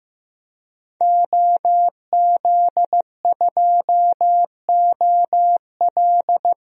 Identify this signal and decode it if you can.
Morse code, heard as a keyed tone, about 15 wpm, 700 Hz. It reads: OZ2OL
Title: DXCC-100 Call Signs - 5 Characters - 15wpm